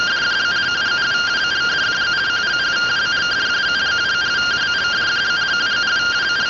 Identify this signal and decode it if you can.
Two-tone teleprinter signal, about 170 Hz shift.